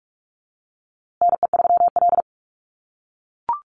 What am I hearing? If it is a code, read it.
Morse code, heard as a keyed tone, about 44 wpm, 700 Hz. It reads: NE3L
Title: USA Call Signs Repeated 3x 45wpm